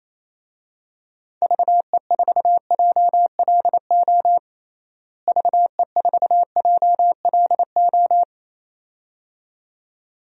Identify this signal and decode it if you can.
Morse code, heard as a keyed tone, about 27 wpm, 700 Hz. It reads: VE4JLO VE4JLO
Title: DXCC-100 Call Signs - 6 Characters - Repeated 2x - 28wpm